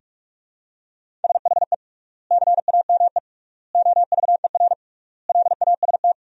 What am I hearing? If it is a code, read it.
Morse code, heard as a keyed tone, about 47 wpm, 700 Hz. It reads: SHE CAME OVER LAST
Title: 3x Word-Spacing - Sentences from Top 300 Words - 45wpm